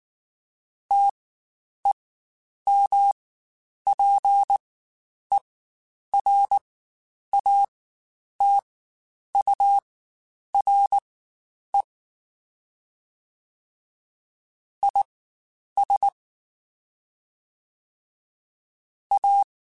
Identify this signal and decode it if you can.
Morse code, heard as a keyed tone, about 19 wpm, spaced as for 9 wpm, 800 Hz. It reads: TEMPERATURE IS A
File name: Spaced Characters / Spaced Characters-mp3s / Weather 4 14 sp @18